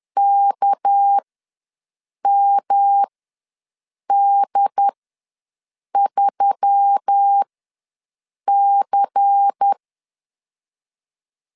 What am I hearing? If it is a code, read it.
Morse code, heard as a keyed tone, about 10 wpm, spaced as for 6 wpm, 800 Hz. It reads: KMD3C
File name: LEZIONI / 8 YJ45 / 8 Y ws 10wpm 3952